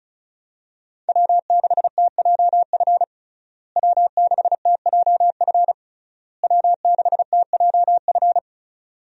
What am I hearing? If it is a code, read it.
Morse code, heard as a keyed tone, about 35 wpm, 700 Hz. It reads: W6TJF W6TJF W6TJF
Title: USA Call Signs - 5 Characters - Repeated 3x - 35wpm